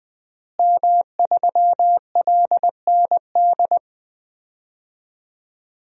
Morse code, heard as a keyed tone, about 20 wpm, 700 Hz. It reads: M3LND
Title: DXCC-100 Call Signs - 5 Characters - Repeated 3x - 20wpm